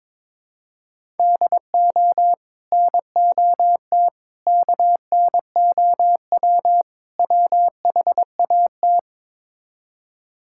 Morse code, heard as a keyed tone, about 22 wpm, 700 Hz. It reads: DO NOT KNOW WHAT
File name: practice / 22wpm / Build A Phrase - 5-Word Phrases - 22wpm